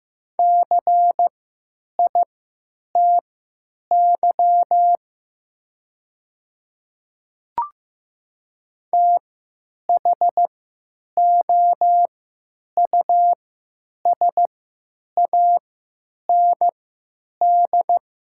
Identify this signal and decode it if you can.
Morse code, heard as a keyed tone, about 15 wpm, spaced as for 9 wpm, 700 Hz. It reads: CITY THOUSAND
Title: 3x Character Spacing - Top 500 Words - 15wpm